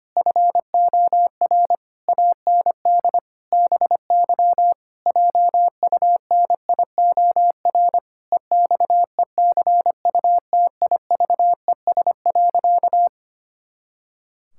Morse code, heard as a keyed tone, about 25 wpm, 700 Hz. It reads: FOR AND BY JUNIOR EXECUTIVES.
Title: Pangram Challenge v2 - 25wpm